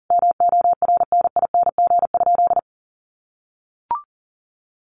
Morse code, heard as a keyed tone, about 40 wpm, 700 Hz. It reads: MORNING?
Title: Sentences from Top 1200 Words 40wpm